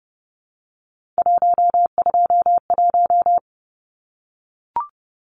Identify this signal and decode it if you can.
Morse code, heard as a keyed tone, about 30 wpm, 700 Hz. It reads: 121
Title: Numbers - 3-Digit 30wpm